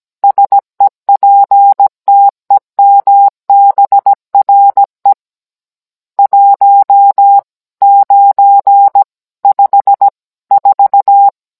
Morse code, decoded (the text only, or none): SEPTEMBRE 1954